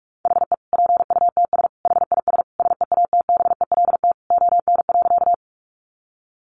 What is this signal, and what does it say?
Morse code, heard as a keyed tone, about 44 wpm, 700 Hz. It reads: HE PUTS HIS SEATBELT ON.
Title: Additional Easy Phrases 45wpm